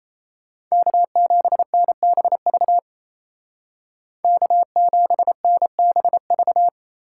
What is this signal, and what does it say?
Morse code, heard as a keyed tone, about 33 wpm, 700 Hz. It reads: K7NBV K7NBV
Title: Speed Racing - USA 1x3 Call Signs 28wpm